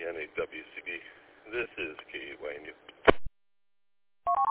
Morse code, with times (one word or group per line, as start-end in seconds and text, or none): none